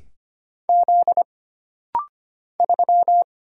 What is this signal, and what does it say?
Morse code, heard as a keyed tone, about 25 wpm, 700 Hz. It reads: Z 3
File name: Single Character (Letter:Number:Punctuation:Prosign) - Rapid-Fire - 25wpm